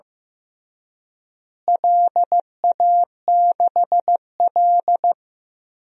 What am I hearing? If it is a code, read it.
Morse code, heard as a keyed tone, about 15 wpm, 700 Hz. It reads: LA6L
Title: DXCC-100 Call Signs - 4 Characters - Repeated 3x - 15wpm